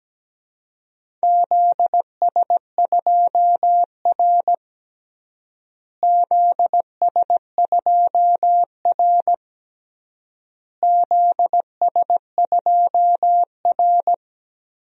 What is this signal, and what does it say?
Morse code, heard as a keyed tone, about 17 wpm, 700 Hz. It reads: ZS2R ZS2R ZS2R
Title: DXCC-100 Call Signs - 4 Characters - Repeated 3x - 17wpm